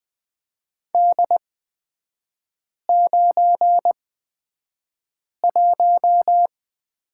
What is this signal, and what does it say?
Morse code, heard as a keyed tone, about 20 wpm, spaced as for 5 wpm, 700 Hz. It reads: D91
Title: 255 Lesson 41e - Period - Three Characters 20wpm